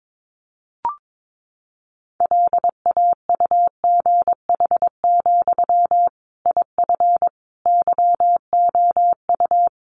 Morse code, heard as a keyed tone, about 22 wpm, 700 Hz. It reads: LAUGH, IF YOU
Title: Sentences from Top 500 Words 22wpm